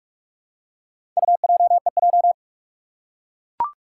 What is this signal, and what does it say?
Morse code, heard as a keyed tone, about 44 wpm, 700 Hz. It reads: UJEJ
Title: Four Challenging Letter-Number Combinations 45wpm